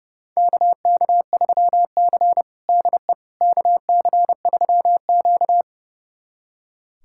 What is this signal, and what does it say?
Morse code, heard as a keyed tone, about 32 wpm, 700 Hz. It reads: KK3C DE KC3Q
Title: USA Callsign DE Callsign - 4 Characters - 30wpm